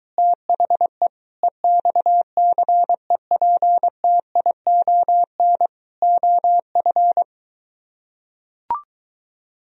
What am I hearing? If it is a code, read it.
Morse code, heard as a keyed tone, about 23 wpm, 700 Hz. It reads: THE EXCEPTION OF